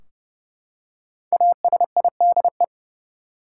Morse code, heard as a keyed tone, about 31 wpm, 700 Hz. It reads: ASIDE
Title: Five Letter Words - 30wpm